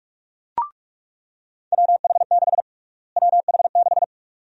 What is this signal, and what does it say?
Morse code, heard as a keyed tone, about 46 wpm, 700 Hz. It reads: WH6 WH6
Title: USA Call Sign Prefixes - Repeated 2x - 45wpm